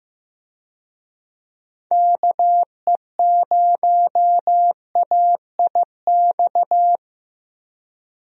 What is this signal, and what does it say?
Morse code, heard as a keyed tone, about 15 wpm, 700 Hz. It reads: KE0AIX
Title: USA Call Signs - 6 Characters - Repeated 2x - 15wpm